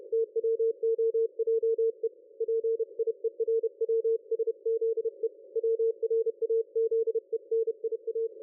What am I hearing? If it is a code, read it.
Morse code, heard as a keyed tone, about 31 wpm, 450 Hz. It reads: TWOJE PIERWSZE WRAZENIA
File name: dlazaawansowanych / nagraniazpasma / ExpertSDR2 line out 23.01.2020 14.22.53